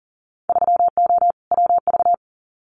Morse code, heard as a keyed tone, about 41 wpm, 700 Hz. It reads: 3O WV
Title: Winter Field Day Exchanges 40wpm